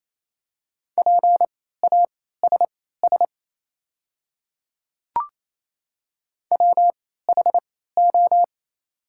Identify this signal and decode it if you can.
Morse code, heard as a keyed tone, about 28 wpm, spaced as for 16 wpm, 700 Hz. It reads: PASS WHO
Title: 3x Character Spacing - Top 500 Words - 28wpm